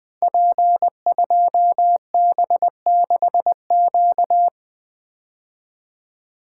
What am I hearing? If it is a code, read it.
Morse code, heard as a keyed tone, about 20 wpm, 700 Hz. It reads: P2B6Q
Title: Code Groups - 5 Letter:Numbers - 20wpm